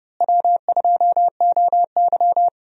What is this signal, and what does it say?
Morse code, heard as a keyed tone, about 30 wpm, 700 Hz. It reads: W2OY
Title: USA Call Signs - 4 Characters - Repeated 3x - 30wpm